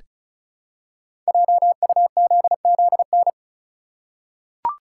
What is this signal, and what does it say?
Morse code, heard as a keyed tone, about 36 wpm, 700 Hz. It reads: JUZZN
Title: Code Groups - 5 Letters - 35wpm